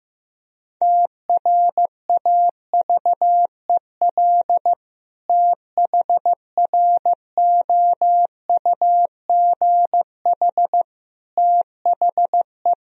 Morse code, heard as a keyed tone, about 16 wpm, 700 Hz. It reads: TRAVEL THROUGH THE
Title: Sentences from Top 1000 Words - Review Part 2 - 15wpm